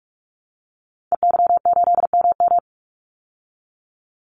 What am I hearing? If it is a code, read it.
Morse code, heard as a keyed tone, about 44 wpm, 700 Hz. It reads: EY8MM
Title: Call Signs Repeated 2x - 45wpm